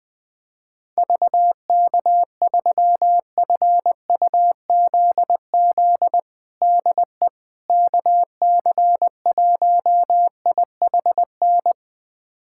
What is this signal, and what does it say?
Morse code, heard as a keyed tone, about 21 wpm, 700 Hz. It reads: VK3FUZZ DE KC1IHN
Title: USA Callsign DE Callsign 20wpm